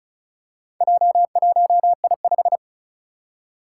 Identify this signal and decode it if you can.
Morse code, heard as a keyed tone, about 36 wpm, 700 Hz. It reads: J1I5
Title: Four Challenging Letter-Number Combinations 35wpm